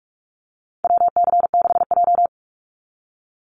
Morse code, heard as a keyed tone, about 44 wpm, 700 Hz. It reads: WC6J
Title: NA SOTA Activator Callsigns 45wpm